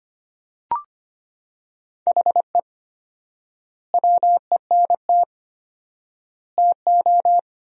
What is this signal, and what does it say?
Morse code, heard as a keyed tone, about 25 wpm, 700 Hz. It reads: HE WENT TO
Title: 4x Word-Spacing - Sentences from Top 200 Words - 25wpm